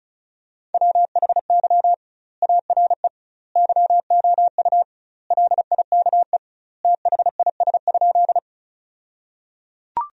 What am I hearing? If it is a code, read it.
Morse code, heard as a keyed tone, about 36 wpm, 700 Hz. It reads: WHY ARE YOU LIKE THIS?
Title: 2x Word-Spacing - Sentences from Top 200 Words - 35wpm